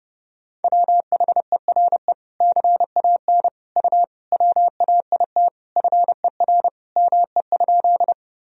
Morse code, heard as a keyed tone, about 30 wpm, 700 Hz. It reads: WHERE CAN U WAIT FER ME?
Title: Sentences from Top 500 Words and Abbreviations 30wpm